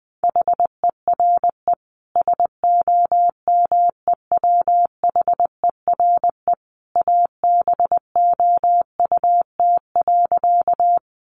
Morse code, heard as a keyed tone, about 20 wpm, 700 Hz. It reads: HERE SOMEWHERE ABOUT.
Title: Sentences from Top 1400 Words 20wpm